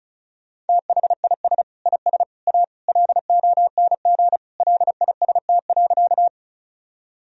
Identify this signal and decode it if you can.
Morse code, heard as a keyed tone, about 36 wpm, 700 Hz. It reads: THIS IS A LONG LIST.
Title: Sentences from Top 1000 Words - Review Part 2 - 35wpm